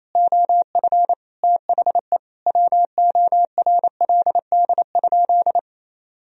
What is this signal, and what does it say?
Morse code, heard as a keyed tone, about 28 wpm, 700 Hz. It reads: OF THE WORLD?
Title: Sentences from Top 200 Words and Abbreviations 28wpm